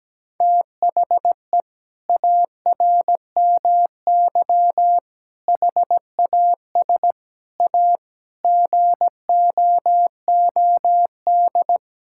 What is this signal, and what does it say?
Morse code, heard as a keyed tone, about 17 wpm, 700 Hz. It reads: THE ARMY HAS A GOOD
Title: Sentences from Top 1000 Words - Review Part 5 - 17wpm